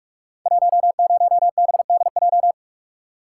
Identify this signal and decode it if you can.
Morse code, heard as a keyed tone, about 44 wpm, 700 Hz. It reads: 10BDJ
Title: Code Groups - 5 Letter:Numbers - 45wpm